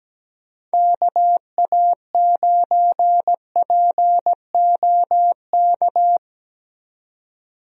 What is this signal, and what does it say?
Morse code, heard as a keyed tone, about 17 wpm, 700 Hz. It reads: KA9POK
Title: USA Call Signs - 6 Characters - Repeated 3x - 17wpm